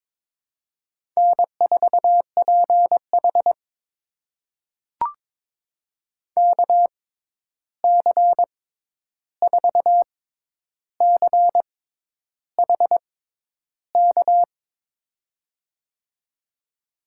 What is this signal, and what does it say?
Morse code, heard as a keyed tone, about 22 wpm, spaced as for 10 wpm, 700 Hz. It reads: N4PH KC4CHK
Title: USA Call Signs - 6x Character Spacing - 22wpm